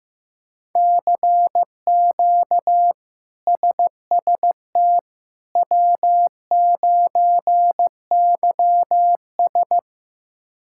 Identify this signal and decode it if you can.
Morse code, heard as a keyed tone, about 15 wpm, 700 Hz. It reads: CQ SST W9YS
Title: K1USN SST QSO Practice - 15wpm